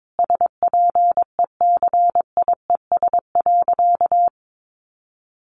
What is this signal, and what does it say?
Morse code, heard as a keyed tone, about 22 wpm, 700 Hz. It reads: SPECIES.
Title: A Sound Of Thunder 22wpm